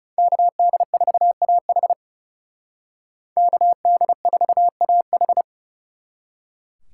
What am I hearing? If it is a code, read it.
Morse code, heard as a keyed tone, about 35 wpm, 700 Hz. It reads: KD4AH KD4AH
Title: Speed Racing - USA 2x2 Call Signs 30wpm